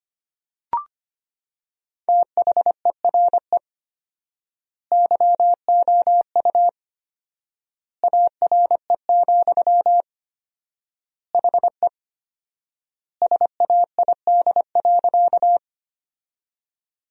Morse code, heard as a keyed tone, about 25 wpm, 700 Hz. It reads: THERE YOU ARE, HE SAID.
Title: 4x Word-Spacing - Sentences from Top 100 Words - 25wpm